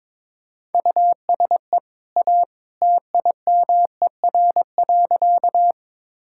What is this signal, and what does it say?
Morse code, heard as a keyed tone, about 22 wpm, 690 Hz. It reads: USE A TIMER.
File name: practice / 22wpm / Easy Sentences with Hints - 22wpm